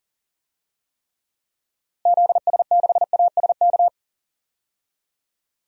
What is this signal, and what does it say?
Morse code, heard as a keyed tone, about 37 wpm, 700 Hz. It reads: ZS6ASK
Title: DXCC-100 Call Signs - 6 Characters - Repeated 3x - 40wpm